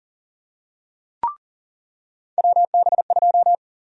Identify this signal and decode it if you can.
Morse code, heard as a keyed tone, about 40 wpm, 700 Hz. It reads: WB2
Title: USA Call Sign Prefixes - 40wpm